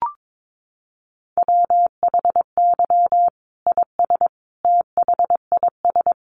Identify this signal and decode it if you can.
Morse code, heard as a keyed tone, about 22 wpm, 700 Hz. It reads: WHY IS THIS
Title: Sentences from Top 1200 Words 22wpm